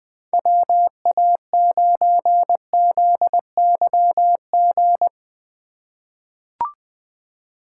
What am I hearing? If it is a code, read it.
Morse code, heard as a keyed tone, about 20 wpm, 700 Hz. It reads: WA9ZYG
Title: USA Call Signs - 6 Characters - 6x Character Spacing - 20wpm